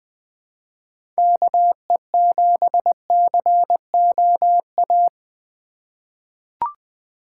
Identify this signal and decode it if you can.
Morse code, heard as a keyed tone, about 20 wpm, 700 Hz. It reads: KE7COA
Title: USA Call Signs - 6 Characters - 3x Character Spacing - 20wpm